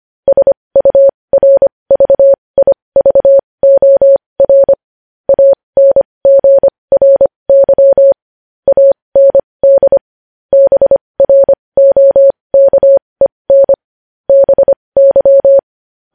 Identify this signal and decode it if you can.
Morse code, heard as a keyed tone, about 25 wpm, spaced as for 20 wpm, 550 Hz. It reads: SURVIVOR ANGRY AND BROKEN BY
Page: news-headlines-in-morse-code-at-20-wpm